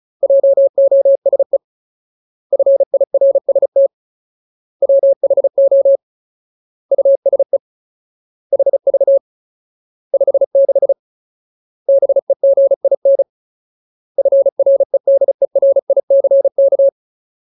Morse code, heard as a keyed tone, about 35 wpm, 550 Hz. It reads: JOSE FIRST WHO USE HV 56 BEGIN FREDERICK
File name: Ultimate Word List / 35WPM Ultimate